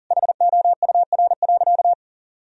Morse code, heard as a keyed tone, about 40 wpm, 700 Hz. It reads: HOUR.
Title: Sentences from Top 1000 Words - Review Part 2 - 40wpm